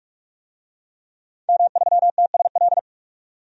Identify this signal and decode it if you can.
Morse code, heard as a keyed tone, about 46 wpm, 700 Hz. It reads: M3TSL
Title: DXCC-100 Call Signs - 5 Characters - 45wpm